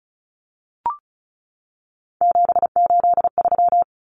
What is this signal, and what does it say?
Morse code, heard as a keyed tone, about 34 wpm, 700 Hz. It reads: 783
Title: Numbers - 3-Digit 35wpm